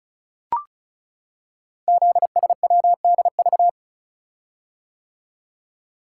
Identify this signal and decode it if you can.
Morse code, heard as a keyed tone, about 36 wpm, 690 Hz. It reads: ZSWDV
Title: Code Groups - 5 Letters - 35wpm